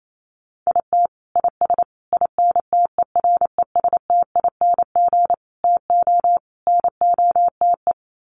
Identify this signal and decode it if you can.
Morse code, heard as a keyed tone, about 28 wpm, 700 Hz. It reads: IT IS INTERESTING TO NOTE
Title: Sets of 5 Words Encore2 28wpm